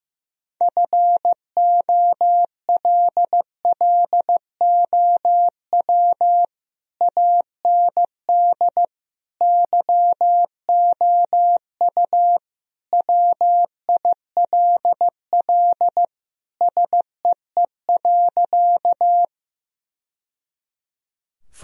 Morse code, heard as a keyed tone, about 16 wpm, 700 Hz. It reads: FOLLOW AND YOU WILL SEE.
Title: Sentences from Top 1000 Words - Review Part 1 - 15wpm